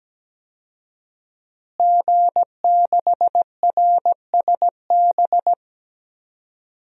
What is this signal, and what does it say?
Morse code, heard as a keyed tone, about 17 wpm, 700 Hz. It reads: G6RSB
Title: DXCC-100 Call Signs - 5 Characters - Repeated 3x - 17wpm